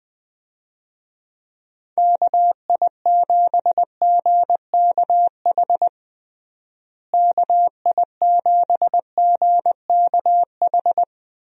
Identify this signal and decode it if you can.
Morse code, heard as a keyed tone, about 20 wpm, 700 Hz. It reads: KI7GKH KI7GKH
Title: USA Call Signs - 6 Characters - Repeated 2x - 20wpm